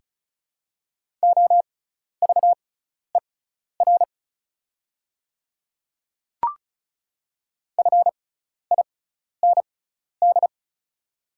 Morse code, heard as a keyed tone, about 36 wpm, spaced as for 12 wpm, 700 Hz. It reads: OVER FIND
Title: 6x Character Spacing - Top 100 Words - 35wpm